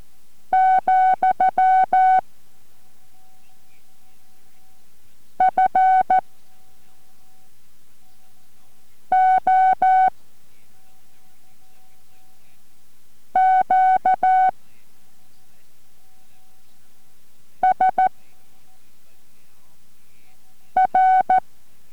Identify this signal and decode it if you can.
Morse code, heard as a keyed tone, about 14 wpm, 750 Hz. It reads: , F O Q S R